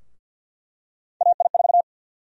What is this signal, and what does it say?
Morse code, heard as a keyed tone, about 52 wpm, 700 Hz. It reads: AI4
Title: USA Call Sign Prefixes - 50wpm